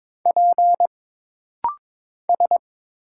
Morse code, heard as a keyed tone, about 22 wpm, 700 Hz. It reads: P S
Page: This rapid-fire timing is geared to push your ICR (Instant Character Recognition) skills.